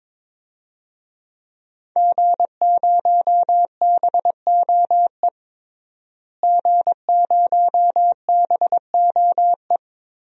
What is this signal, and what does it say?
Morse code, heard as a keyed tone, about 22 wpm, 700 Hz. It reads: G0BOE G0BOE
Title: DXCC-100 Call Signs - 5 Characters - Repeated 2x - 22wpm